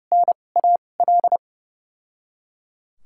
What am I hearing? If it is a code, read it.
Morse code, heard as a keyed tone, about 30 wpm, spaced as for 22 wpm, 700 Hz. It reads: NAL
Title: 2x Character Spacing - Top 500 Words - 30wpm